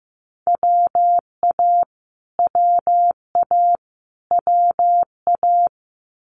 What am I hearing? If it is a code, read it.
Morse code, heard as a keyed tone, about 15 wpm, 700 Hz. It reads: WA WA WA
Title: Multi-letter Phonemes Repeated 3x 15wpm